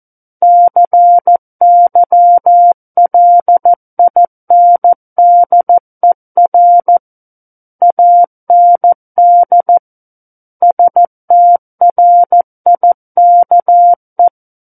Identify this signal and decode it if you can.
Morse code, heard as a keyed tone, about 14 wpm, 700 Hz. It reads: CYLINDER AND STRIKE
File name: Book One / 05-Chapter 5 - 14 WPM